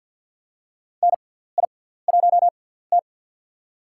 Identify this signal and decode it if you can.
Morse code, heard as a keyed tone, about 47 wpm, spaced as for 17 wpm, 700 Hz. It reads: NI1T
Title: USA Call Signs - 6x Character Spacing - 50wpm